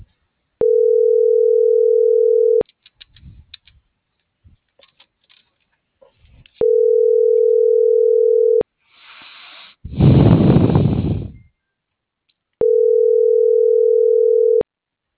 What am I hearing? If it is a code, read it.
Ringback tone (North America), 440 + 480 Hz.